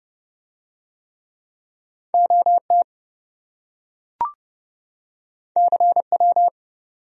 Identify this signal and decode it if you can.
Morse code, heard as a keyed tone, about 27 wpm, 700 Hz. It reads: OT CW